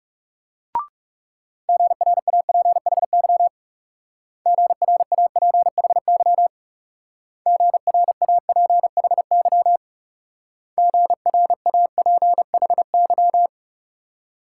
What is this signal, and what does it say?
Morse code, heard as a keyed tone, about 38 wpm, 700 Hz. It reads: GRAPHY GRAPHY GRAPHY GRAPHY